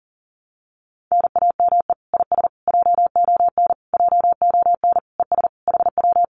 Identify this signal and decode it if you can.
Morse code, heard as a keyed tone, about 37 wpm, 700 Hz. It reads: NAME IS JON JON ES HW
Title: Common QSO Phrases Encore 40wpm